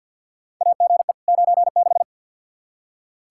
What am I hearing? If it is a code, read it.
Morse code, heard as a keyed tone, about 51 wpm, 700 Hz. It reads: AGE 96
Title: Practice QSO - Exchange 3 - More Info - 50wpm